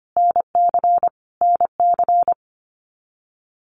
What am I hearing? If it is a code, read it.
Morse code, heard as a keyed tone, about 25 wpm, 700 Hz. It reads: NC NC
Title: US States Repeated 3x 25wpm